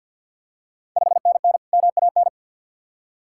Morse code, heard as a keyed tone, about 49 wpm, 690 Hz. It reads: 5NN MAN
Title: Ontario QSO Party Exchange 50wpm